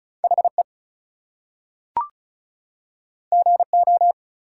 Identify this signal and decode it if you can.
Morse code, heard as a keyed tone, about 36 wpm, 700 Hz. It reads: HE GO
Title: Two Letter Words - 35wpm